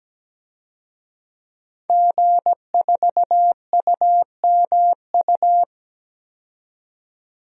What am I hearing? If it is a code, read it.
Morse code, heard as a keyed tone, about 17 wpm, 700 Hz. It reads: G4UMU
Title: DXCC-100 Call Signs - 5 Characters - 17wpm